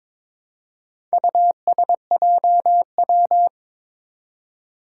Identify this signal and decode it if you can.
Morse code, heard as a keyed tone, about 22 wpm, 700 Hz. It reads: USJW
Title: Four Challenging Letter-Number Combinations 22wpm